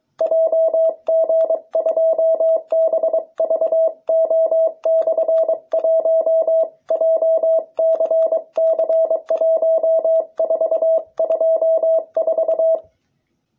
Morse code, heard as a keyed tone, about 23 wpm, 660 Hz. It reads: JG2BVO/1 JCC1424